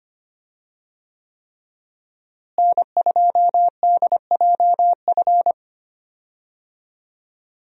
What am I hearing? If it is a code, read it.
Morse code, heard as a keyed tone, about 25 wpm, 700 Hz. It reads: N2DJF
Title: USA Call Signs - 5 Characters - Repeated 3x - 25wpm